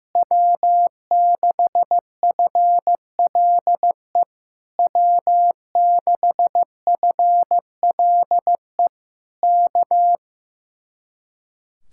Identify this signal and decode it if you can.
Morse code, heard as a keyed tone, about 15 wpm, 700 Hz. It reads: W6FLE W6FLE K
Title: Speed-Racing Common QSO Phrases Encore - 15wpm